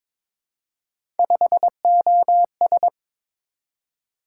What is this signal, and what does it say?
Morse code, heard as a keyed tone, about 22 wpm, 700 Hz. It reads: 5OS
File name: practice / 22wpm / Three Letter-Number Combinations - 22wpm